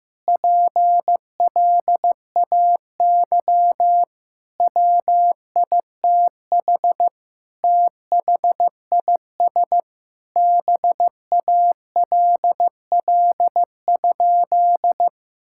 Morse code, heard as a keyed tone, about 15 wpm, 700 Hz. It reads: PLAY WITH THIS BALL?
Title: Sentences from Top 1000 Words - Review Part 6 - 15wpm